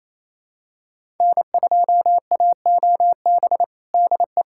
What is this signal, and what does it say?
Morse code, heard as a keyed tone, about 28 wpm, 700 Hz. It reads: N2AOB DE
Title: USA Callsign DE Callsign - 5 Characters - 28wpm